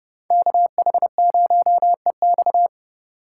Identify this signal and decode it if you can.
Morse code, heard as a keyed tone, about 36 wpm, 700 Hz. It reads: KH0EX
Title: Speed Racing - USA 2x2 Call Signs 30wpm